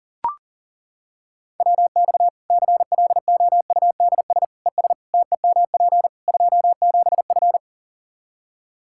Morse code, heard as a keyed tone, about 40 wpm, 700 Hz. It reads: WX CLOUDS ES TEMP 27F
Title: Practice QSO - Exchange 2 - Info - 40wpm